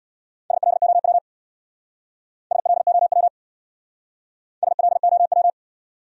Morse code, heard as a keyed tone, about 59 wpm, 700 Hz. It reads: SLOW SLOW SLOW
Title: Speed-Racing Top 500 Words 50wpm